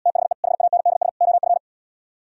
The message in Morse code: THE HANDS OF

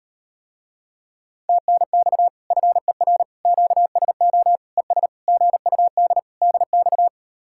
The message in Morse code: TNX FER QSO ES GUD DX

